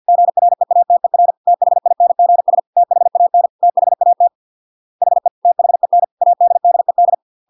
Morse code, heard as a keyed tone, about 50 wpm, 700 Hz. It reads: GREATER THINGS THAN THAT HE THEN ADDED